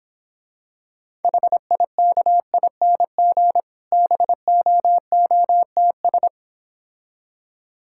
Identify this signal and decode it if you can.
Morse code, heard as a keyed tone, about 26 wpm, 700 Hz. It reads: HIKING BOOTS